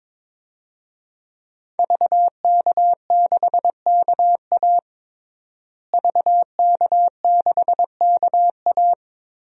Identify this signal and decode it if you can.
Morse code, heard as a keyed tone, about 22 wpm, 700 Hz. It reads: VK6KA VK6KA
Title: DXCC-100 Call Signs - 5 Characters - Repeated 2x - 22wpm